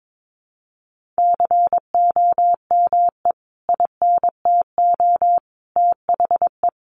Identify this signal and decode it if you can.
Morse code, heard as a keyed tone, about 22 wpm, 700 Hz. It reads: COME INTO THE
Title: Sentences from Top 1400 Words 22wpm